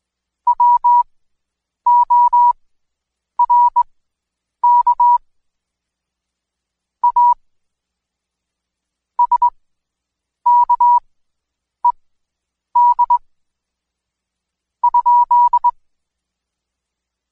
Morse code, heard as a keyed tone, about 20 wpm, spaced as for 8 wpm, 970 Hz. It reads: WORK A SKED ?
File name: Code Quick / AudioDownld2 / Seg114-Day19-DayDrill 19